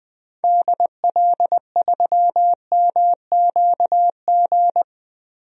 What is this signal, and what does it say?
Morse code, heard as a keyed tone, about 20 wpm, 700 Hz. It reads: DL3MQG
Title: DXCC-100 Call Signs - 6 Characters - Repeated 3x - 20wpm